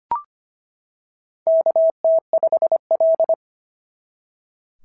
Morse code, heard as a keyed tone, about 23 wpm, 650 Hz. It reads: KT5L
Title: SOTA QSO Practice 25wpm